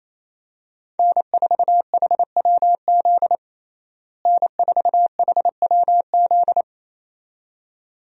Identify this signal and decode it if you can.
Morse code, heard as a keyed tone, about 28 wpm, 700 Hz. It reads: N4HWZ N4HWZ
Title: USA Call Signs - 5 Characters - Repeated 2x - 28wpm